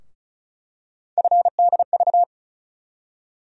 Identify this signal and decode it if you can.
Morse code, heard as a keyed tone, about 36 wpm, 700 Hz. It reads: FDV
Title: USA Call Sign Postfixes - 3 Letters 35wpm